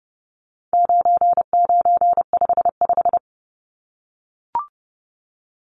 Morse code, heard as a keyed tone, about 30 wpm, 700 Hz. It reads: 9955
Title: Numbers - 4-Digit 30wpm